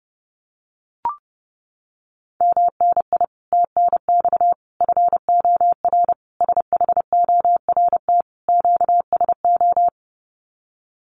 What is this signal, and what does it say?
Morse code, heard as a keyed tone, about 28 wpm, 700 Hz. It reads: MNI TNX FOR SHORT QSO
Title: Common QSO Phrases 30wpm